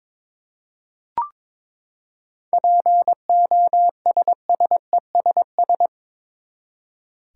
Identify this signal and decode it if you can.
Morse code, heard as a keyed tone, about 23 wpm, 700 Hz. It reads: POSSESS
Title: Seven Letter Words - 22wpm